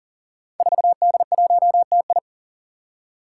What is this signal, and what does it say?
Morse code, heard as a keyed tone, about 41 wpm, 700 Hz. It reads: 4D1TI
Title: Five Challenging Letter-Number Combinations 40wpm